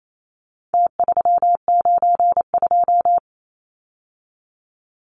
Morse code, heard as a keyed tone, about 28 wpm, 700 Hz. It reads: T392
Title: Numbers - Contest Style - 4-Digit 28wpm